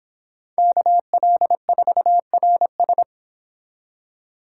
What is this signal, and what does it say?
Morse code, heard as a keyed tone, about 26 wpm, 700 Hz. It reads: KL4RS